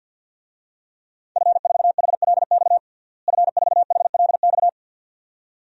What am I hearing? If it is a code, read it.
Morse code, heard as a keyed tone, about 50 wpm, 700 Hz. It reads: F4HLX F4HLX
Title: DXCC-100 Call Signs - 5 Characters - Repeated 2x - 50wpm